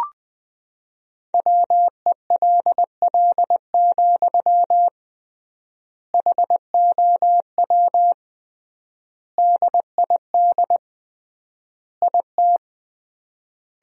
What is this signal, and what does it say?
Morse code, heard as a keyed tone, about 21 wpm, 700 Hz. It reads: WELL, HOW DID IT
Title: 3x Word-Spacing - Sentences from Top 200 Words - 20wpm